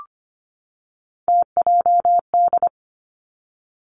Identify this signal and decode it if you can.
Morse code, heard as a keyed tone, about 25 wpm, 700 Hz. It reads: TJD
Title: Three Challenging Letter-Number Combinations 25wpm